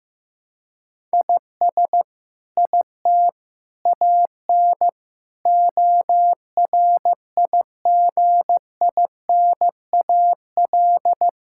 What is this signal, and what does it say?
Morse code, heard as a keyed tone, about 15 wpm, 700 Hz. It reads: IS IT AN ORIGINAL
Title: Sentences from Top 1000 Words - Review Part 6 - 15wpm